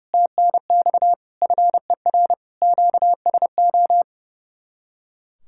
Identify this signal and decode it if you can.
Morse code, heard as a keyed tone, about 38 wpm, 700 Hz. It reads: TNX FER QSO